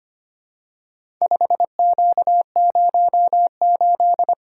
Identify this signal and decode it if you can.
Morse code, heard as a keyed tone, about 25 wpm, 700 Hz. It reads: 5Q08
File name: practice / 25wpm / Four Letter-Number Combinations - 25wpm